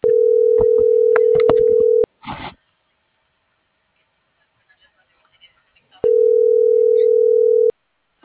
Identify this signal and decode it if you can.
Ringback tone (North America), 440 + 480 Hz.
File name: CENTRAL MAYAB / out brandon.fi